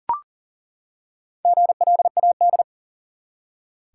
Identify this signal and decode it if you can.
Morse code, heard as a keyed tone, about 37 wpm, 700 Hz. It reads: GLAD
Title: Four Letter Words - 40wpm